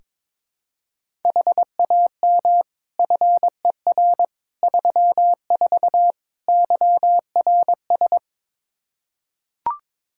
Morse code, heard as a keyed tone, about 22 wpm, 700 Hz. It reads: HAM FER 34 YRS
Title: Practice QSO - Exchange 3 - More Info - 22wpm